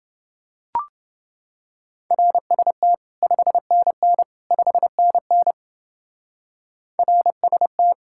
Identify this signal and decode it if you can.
Morse code, heard as a keyed tone, about 30 wpm, 700 Hz. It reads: RST 5NN 5NN RST